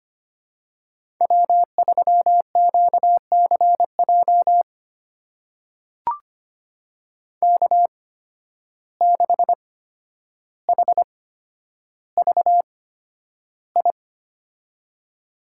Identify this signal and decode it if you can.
Morse code, heard as a keyed tone, about 25 wpm, spaced as for 11 wpm, 700 Hz. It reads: W3QCJ K6HVI
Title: USA Call Signs - 5 Characters - 8x Character Spacing - 25wpm